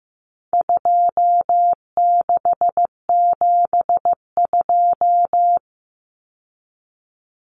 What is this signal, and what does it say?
Morse code, heard as a keyed tone, about 15 wpm, 700 Hz. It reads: 2672
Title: Numbers - 4-Digit 15wpm